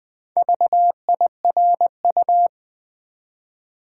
Morse code, heard as a keyed tone, about 20 wpm, 700 Hz. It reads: VIRU
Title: Build A Word - 1 Letter At A Time - 5-Letter Words - 20wpm